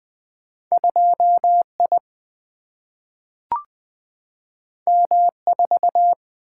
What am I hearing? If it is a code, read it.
Morse code, heard as a keyed tone, about 20 wpm, 700 Hz. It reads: 2I M4
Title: Single-Letter Single-Number 20wpm